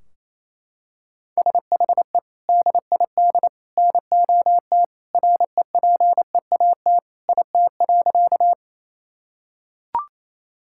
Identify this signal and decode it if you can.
Morse code, heard as a keyed tone, about 28 wpm, 700 Hz. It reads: SHE DID NOT REPEAT IT.
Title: Sentences from Top 1600 Words 28wpm